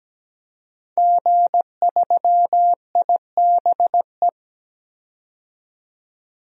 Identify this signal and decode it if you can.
Morse code, heard as a keyed tone, about 17 wpm, 700 Hz. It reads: G3IBE
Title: DXCC-100 Call Signs - 5 Characters - 17wpm